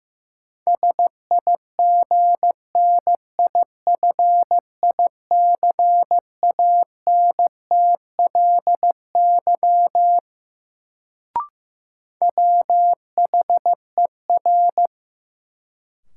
Morse code, heard as a keyed tone, about 16 wpm, 700 Hz. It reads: SIGNIFICANTLY WHER
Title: Sentences from Top 1500 Words - New Word Drill Alt - 15wpm